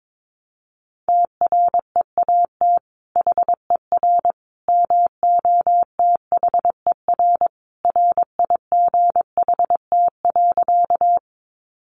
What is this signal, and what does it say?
Morse code, heard as a keyed tone, about 22 wpm, 700 Hz. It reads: TREAT HER MOTHER RIGHT.
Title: Sentences from Top 1300 Words 22wpm